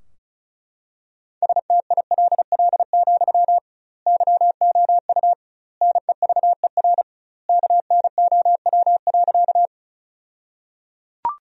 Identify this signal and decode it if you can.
Morse code, heard as a keyed tone, about 36 wpm, 700 Hz. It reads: STILL, YOU NEVER KNOW.
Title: 2x Word-Spacing - Sentences from Top 300 Words - 35wpm